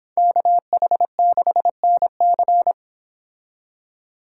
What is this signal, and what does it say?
Morse code, heard as a keyed tone, about 25 wpm, 700 Hz. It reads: KH6NC